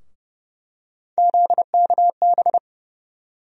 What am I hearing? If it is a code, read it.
Morse code, heard as a keyed tone, about 30 wpm, 700 Hz. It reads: ZKB
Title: USA Call Sign Postfixes - 3 Letters 30wpm